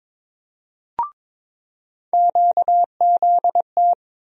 Morse code, heard as a keyed tone, about 22 wpm, 700 Hz. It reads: QZT
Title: USA Call Sign Postfixes - 3 Letters 22wpm